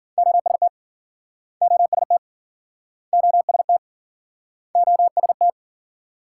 Morse code, heard as a keyed tone, about 53 wpm, 700 Hz. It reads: OST OST OST OST